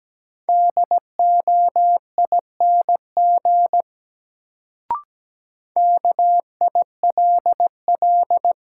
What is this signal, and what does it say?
Morse code, heard as a keyed tone, about 17 wpm, 700 Hz. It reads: DOING KILL
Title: Sentences from Top 1400 Words - New Word Drill Alt - 17wpm